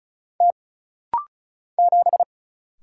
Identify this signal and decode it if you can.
Morse code, heard as a keyed tone, about 36 wpm, 700 Hz. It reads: T 7
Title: Single Letter-Number Alt - Rapid-Fire 35wpm